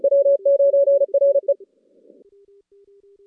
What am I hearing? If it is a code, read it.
Morse code, heard as a keyed tone, about 33 wpm, 560 Hz. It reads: W9RE
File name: contests / CW-SS-k5tr- -2100